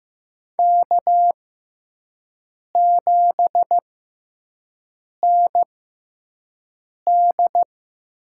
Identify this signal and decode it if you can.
Morse code, heard as a keyed tone, about 15 wpm, spaced as for 5 wpm, 700 Hz. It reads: K7ND
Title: USA Call Signs - 5 Characters - 6x Character Spacing - 15wpm